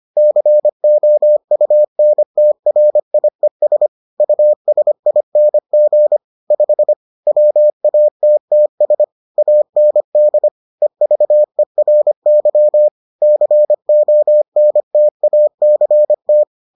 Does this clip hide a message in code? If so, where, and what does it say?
Yes: Morse code from 0.2 to 16.4 s, heard as a keyed tone, about 25 wpm, 600 Hz. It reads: COUNTRIES USING 5 WATTS AND EVERY CONTACT